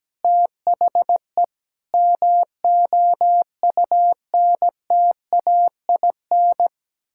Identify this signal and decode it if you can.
Morse code, heard as a keyed tone, about 17 wpm, 700 Hz. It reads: THE MOUNTAIN
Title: Sentences from Top 1000 Words - Review Part 6 - 17wpm